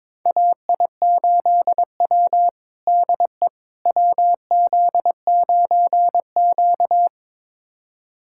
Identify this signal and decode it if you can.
Morse code, heard as a keyed tone, about 21 wpm, 700 Hz. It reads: AI8W DE WZ9Q
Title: USA Callsign DE Callsign - 4 Characters - 22wpm